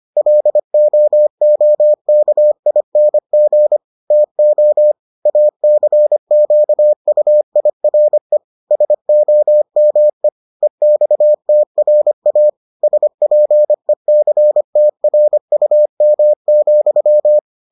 Morse code, heard as a keyed tone, about 25 wpm, 600 Hz. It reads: LOOKING TO ACQUIRE SOME EXTRA SPECTRUM,